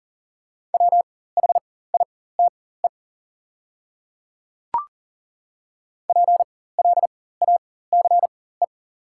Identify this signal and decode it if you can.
Morse code, heard as a keyed tone, about 40 wpm, spaced as for 19 wpm, 700 Hz. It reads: WHITE PLACE
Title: 4x Character Spacing - Top 300 Words - 40wpm